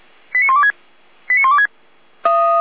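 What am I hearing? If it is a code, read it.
Five-tone selective call (ZVEI), heard as quick runs of five tones, tones of about 70 ms. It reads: 89126 89126
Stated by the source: ZVEI five-tone selective call received by SDR (455).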